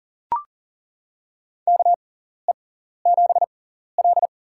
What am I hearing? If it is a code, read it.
Morse code, heard as a keyed tone, about 40 wpm, spaced as for 14 wpm, 700 Hz. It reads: KE7L
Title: USA Call Signs - 6x Character Spacing - 40wpm